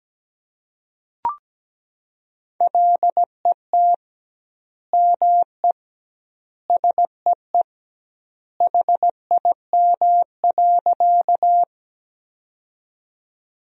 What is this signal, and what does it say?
Morse code, heard as a keyed tone, about 17 wpm, 700 Hz. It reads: LET ME SEE HIM.
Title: 2x Word-Spacing - Sentences from Top 300 Words - 17wpm